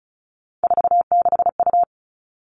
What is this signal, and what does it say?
Morse code, heard as a keyed tone, about 34 wpm, 700 Hz. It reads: 46U
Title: All Cut Numbers - 3-Digit 35wpm